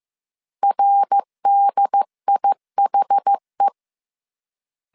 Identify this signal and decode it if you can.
Morse code, heard as a keyed tone, about 15 wpm, 800 Hz. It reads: RDIHE
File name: CW / DET / 1 3 1500C S A 14wpm 2447